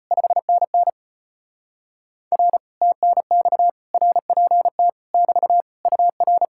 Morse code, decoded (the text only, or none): 5NN R TNX RPT <BT> UR